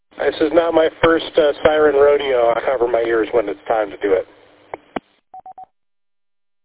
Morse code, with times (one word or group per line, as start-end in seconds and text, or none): none